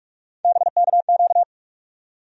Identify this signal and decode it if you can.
Morse code, heard as a keyed tone, about 42 wpm, 700 Hz. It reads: BKQ
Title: USA Call Sign Postfixes - 3 Letters 45wpm